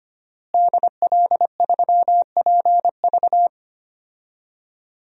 Morse code, heard as a keyed tone, about 25 wpm, 700 Hz. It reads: DL3PV
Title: DXCC-100 Call Signs - 5 Characters - Repeated 2x - 25wpm